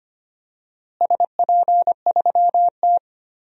Morse code, heard as a keyed tone, about 25 wpm, 700 Hz. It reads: SP3T
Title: DXCC-100 Call Signs - 4 Characters - Repeated 3x - 25wpm